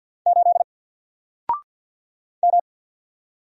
Morse code, heard as a keyed tone, about 48 wpm, 700 Hz. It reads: 8 M
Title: Single Character (Letter:Number:Punctuation:Prosign) 50wpm